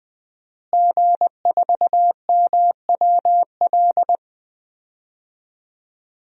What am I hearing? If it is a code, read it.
Morse code, heard as a keyed tone, about 20 wpm, 700 Hz. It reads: G4MWL
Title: DXCC-100 Call Signs - 5 Characters - Repeated 2x - 20wpm